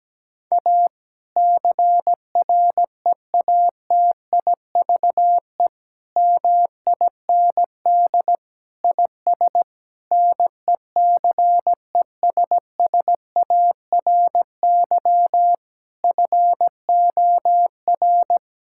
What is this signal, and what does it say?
Morse code, heard as a keyed tone, about 17 wpm, 700 Hz. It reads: A CREATIVE MIND IS NECESSARY FOR
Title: Sentences from Top 1000 Words - Review Part 5 - 17wpm